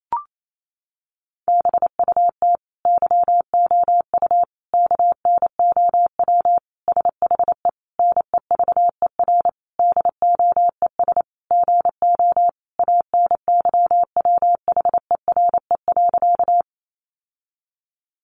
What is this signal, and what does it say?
Morse code, heard as a keyed tone, about 28 wpm, 700 Hz. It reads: BUT YOU KNOW SHE NEVER DOES GO ANYWHERE.
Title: Sentences from Top 1400 Words 28wpm